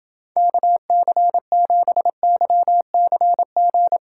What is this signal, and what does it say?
Morse code, heard as a keyed tone, about 29 wpm, 700 Hz. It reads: KC7YCG